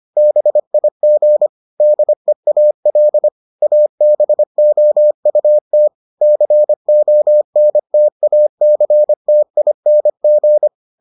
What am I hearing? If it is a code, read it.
Morse code, heard as a keyed tone, about 25 wpm, 600 Hz. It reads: BIG DEAL ABOUT CONTACTING